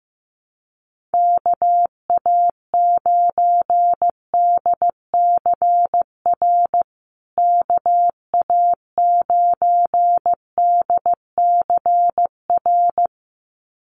Morse code, heard as a keyed tone, about 15 wpm, 700 Hz. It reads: KA9DCR KA9DCR
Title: USA Call Signs Repeated 2x 15wpm